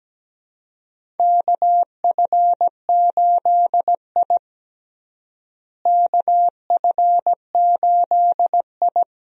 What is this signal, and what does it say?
Morse code, heard as a keyed tone, about 17 wpm, 700 Hz. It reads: KF8I KF8I
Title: USA Call Signs - 4 Characters - Repeated 2x - 17wpm